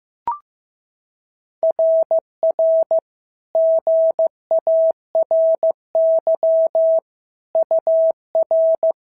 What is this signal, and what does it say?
Morse code, heard as a keyed tone, about 14 wpm, 650 Hz. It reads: RR GARY UR
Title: SOTA QSO Practice 15wpm